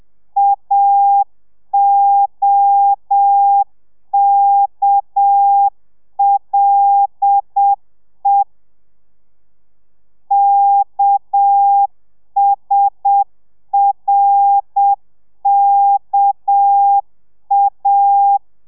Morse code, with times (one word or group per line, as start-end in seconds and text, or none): 0.4-8.4 AOKLE
10.3-18.4 KSRKA